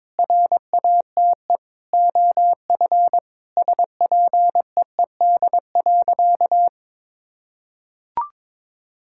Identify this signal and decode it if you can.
Morse code, heard as a keyed tone, about 22 wpm, 700 Hz. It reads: RATE OF SPEED.
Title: Sentences from Top 1000 Words - Review Part 5 - 22wpm